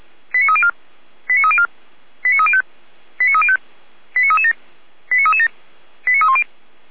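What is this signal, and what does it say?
Five-tone selective call (ZVEI), heard as quick runs of five tones, tones of about 70 ms. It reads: 89304 89304 89305 89305 89337 89337 89310
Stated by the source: ZVEI five-tone call received on an SDR.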